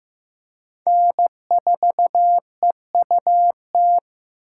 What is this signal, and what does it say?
Morse code, heard as a keyed tone, about 16 wpm, 700 Hz. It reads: N4EUT
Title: USA Call Signs - 5 Characters - 2x Character Spacing - 15wpm